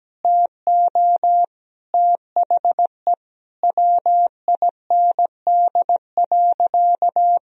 Morse code, heard as a keyed tone, about 17 wpm, 700 Hz. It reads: TO THE WIND.
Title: Sentences from Top 1000 Words - Review Part 2 - 17wpm